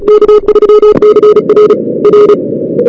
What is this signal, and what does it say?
Morse code, heard as a keyed tone, about 34 wpm, 420 Hz. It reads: K3CR R
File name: CWSprint / Feb2006 / 2006-feb-cw-sprint-k5tr op-n5rz -02:00